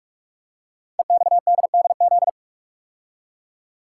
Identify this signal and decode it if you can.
Morse code, heard as a keyed tone, about 44 wpm, 700 Hz. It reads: EXDDZ
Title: Code Groups - 5 Letters - 45wpm